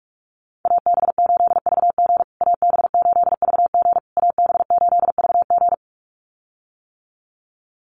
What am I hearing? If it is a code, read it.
Morse code, heard as a keyed tone, about 44 wpm, 690 Hz. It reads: AB8VG AB8VG AB8VG
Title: USA Call Signs Repeated 3x 45wpm